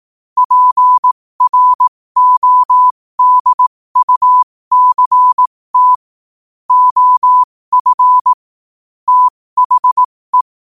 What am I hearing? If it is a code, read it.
Morse code, heard as a keyed tone, about 18 wpm, spaced as for 15 wpm, 1000 Hz. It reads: PRODUCT OF THE
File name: qotdfiles / 15wpm